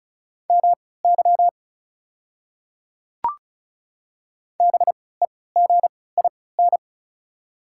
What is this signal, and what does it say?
Morse code, heard as a keyed tone, about 34 wpm, spaced as for 20 wpm, 700 Hz. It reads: MY BEGIN
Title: 3x Character Spacing - Top 300 Words - 35wpm